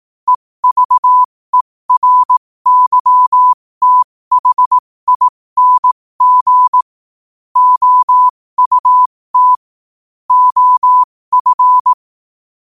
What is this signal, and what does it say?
Morse code, heard as a keyed tone, about 18 wpm, spaced as for 15 wpm, 1000 Hz. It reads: EVERYTHING OUT OF
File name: qotdfiles / 15wpm